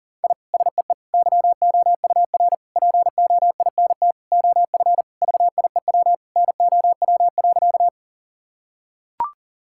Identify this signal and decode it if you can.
Morse code, heard as a keyed tone, about 40 wpm, 700 Hz. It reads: I SEE YOUR POINT OF VIEW NOW.
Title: Sentences from Top 1000 Words - Review Part 6 - 40wpm